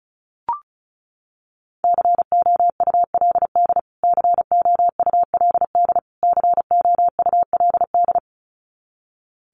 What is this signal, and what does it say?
Morse code, heard as a keyed tone, about 34 wpm, 700 Hz. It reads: COULD COULD COULD
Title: Top 500 Words Repeated 35wpm